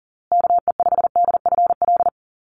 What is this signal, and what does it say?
Morse code, heard as a keyed tone, about 40 wpm, 700 Hz. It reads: KE5DFL
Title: USA Call Signs Repeated 3x 40wpm